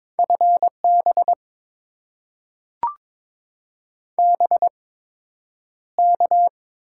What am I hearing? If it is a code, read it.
Morse code, heard as a keyed tone, about 22 wpm, spaced as for 9 wpm, 700 Hz. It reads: FB BK
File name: Top 90 QSO Elements - 8x Character Spacing - 22wpm